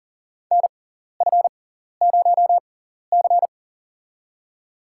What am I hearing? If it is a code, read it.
Morse code, heard as a keyed tone, about 38 wpm, spaced as for 14 wpm, 700 Hz. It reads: NF0C
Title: USA Call Signs - 4 Characters - 6x Character Spacing - 40wpm